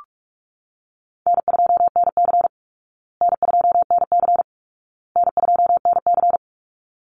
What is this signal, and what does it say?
Morse code, heard as a keyed tone, about 44 wpm, 700 Hz. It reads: N2NC N2NC N2NC
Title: Call Signs Repeated 2x - 45wpm